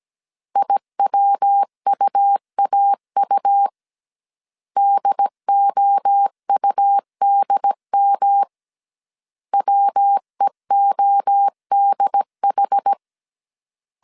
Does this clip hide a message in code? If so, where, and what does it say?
Yes: Morse code from 0.6 to 12.9 s, heard as a keyed tone, about 17 wpm, 800 Hz. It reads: IWUAU DOUDM WEODH